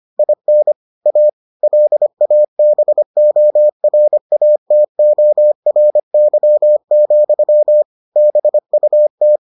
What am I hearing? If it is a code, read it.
Morse code, heard as a keyed tone, about 25 wpm, 600 Hz. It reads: IN A LABORATORY, BUT